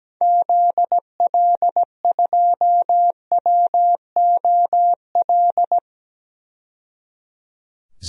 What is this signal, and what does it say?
Morse code, heard as a keyed tone, about 17 wpm, 700 Hz. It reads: ZL2WOL
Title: DXCC-100 Call Signs - 6 Characters - 17wpm